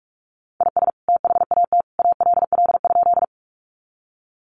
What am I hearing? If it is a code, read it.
Morse code, heard as a keyed tone, about 44 wpm, 700 Hz. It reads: IS THAT ALL?
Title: Sentences from Top 100 Words 45wpm